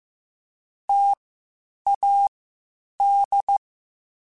Morse code, heard as a keyed tone, about 15 wpm, spaced as for 8 wpm, 800 Hz. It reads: TAD
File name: Difficult Combos / Difficult Combos-mp3s / T AOEI @13